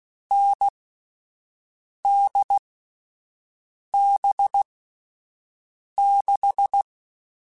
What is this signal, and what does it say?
Morse code, heard as a keyed tone, about 16 wpm, 800 Hz. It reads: N D B 6
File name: Short Runs / Short Runs-mp3s / NDB NDB6 @10